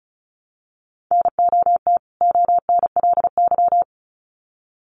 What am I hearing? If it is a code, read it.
Morse code, heard as a keyed tone, about 34 wpm, 700 Hz. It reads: NOT ONLY
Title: Sets of 2 Words 35wpm